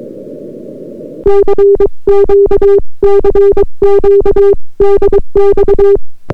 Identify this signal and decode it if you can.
Morse code, heard as a keyed tone, about 23 wpm, 400 Hz. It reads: CQ CQ DX